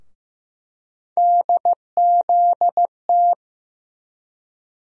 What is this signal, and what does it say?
Morse code, heard as a keyed tone, about 15 wpm, 700 Hz. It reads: DZT
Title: USA Call Sign Postfixes - 3 Letters 15wpm